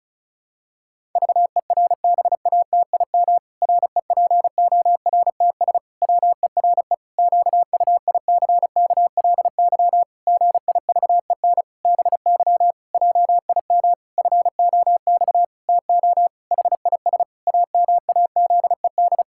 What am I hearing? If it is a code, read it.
Morse code, heard as a keyed tone, about 36 wpm, 700 Hz. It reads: VERBATIM REPORTS WERE QUICKLY GIVEN BY JIM FOX TO HIS AMAZED
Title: Pangram Challenge v2 - 35wpm